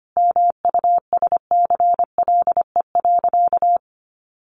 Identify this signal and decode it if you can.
Morse code, heard as a keyed tone, about 25 wpm, 700 Hz. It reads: MUSCLE.
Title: Sentences from Top 1300 Words 25wpm